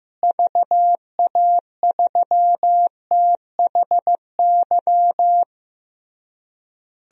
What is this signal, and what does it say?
Morse code, heard as a keyed tone, about 15 wpm, 700 Hz. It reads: VA3THY
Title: DXCC-100 Call Signs - 6 Characters - Repeated 3x - 15wpm